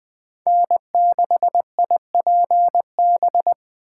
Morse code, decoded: N6IPB

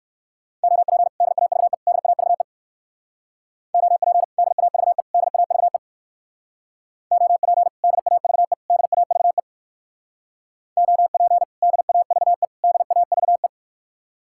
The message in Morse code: OP DAVE DAVE OP DAVE DAVE OP DAVE DAVE OP DAVE DAVE

